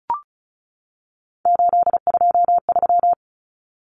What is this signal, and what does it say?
Morse code, heard as a keyed tone, about 34 wpm, 700 Hz. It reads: 823
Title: Common Cut Numbers - 3-Digit 35wpm